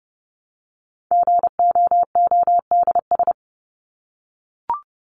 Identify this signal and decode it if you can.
Morse code, heard as a keyed tone, about 30 wpm, 700 Hz. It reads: GOODS
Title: ES:S Suffix - Top 100 Words with Base Words 30wpm